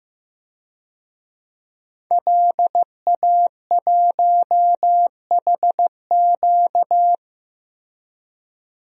Morse code, heard as a keyed tone, about 15 wpm, 700 Hz. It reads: LA1HQ
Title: DXCC-100 Call Signs - 5 Characters - Repeated 3x - 15wpm